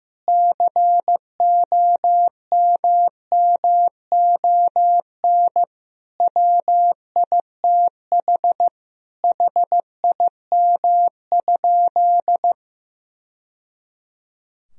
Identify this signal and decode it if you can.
Morse code, heard as a keyed tone, about 15 wpm, 700 Hz. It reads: COMMON WITH HIM?
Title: Sentences from Top 1000 Words - Review Part 3 - 15wpm